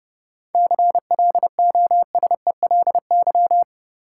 Morse code, heard as a keyed tone, about 30 wpm, 700 Hz. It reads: CLOSELY
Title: Seven Letter Words - 30wpm